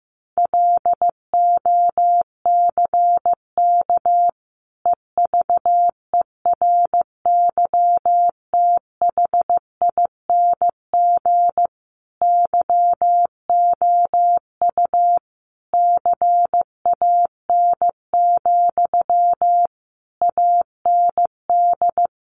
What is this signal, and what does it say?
Morse code, heard as a keyed tone, about 15 wpm, 700 Hz. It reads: LOCK EVERYTHING YOU CAN, AND
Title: Sentences from Top 1300 Words 15wpm